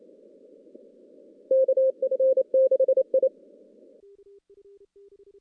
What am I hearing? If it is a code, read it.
Morse code, heard as a keyed tone, about 28 wpm, 1600 Hz, with noise. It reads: KF6I
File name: contests / CW-SS-k5tr- -0000